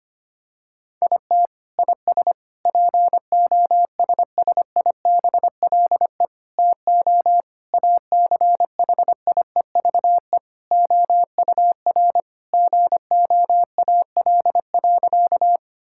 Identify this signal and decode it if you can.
Morse code, heard as a keyed tone, about 25 wpm, 700 Hz. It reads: IT IS POSSIBLE TO ACHIEVE OUR GOAL.
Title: Sentences from Top 1000 Words - Review Part 6 - 25wpm